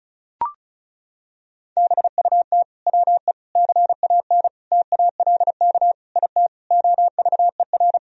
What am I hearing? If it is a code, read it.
Morse code, heard as a keyed tone, about 36 wpm, 700 Hz. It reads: BUT WE CAN TALK IT OVER
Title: Sentences from Top 1000 Words - Review Part 4 - 35wpm